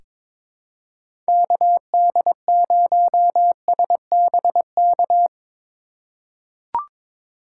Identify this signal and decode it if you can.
Morse code, heard as a keyed tone, about 21 wpm, 700 Hz. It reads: KD0SBK
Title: USA Call Signs - 2x Character Spacing - 22wpm